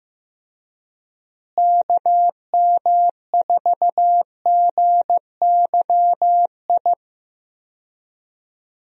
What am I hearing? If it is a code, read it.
Morse code, heard as a keyed tone, about 16 wpm, 700 Hz. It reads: KM4GYI
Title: USA Call Signs - 6 Characters - Repeated 2x - 15wpm